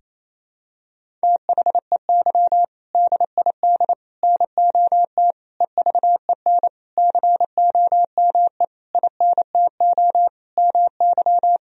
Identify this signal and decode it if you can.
Morse code, heard as a keyed tone, about 28 wpm, 700 Hz. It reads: THEY DID NOT EVEN COME INTO MY
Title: Sentences from Top 1000 Words - Review Part 1 - 28wpm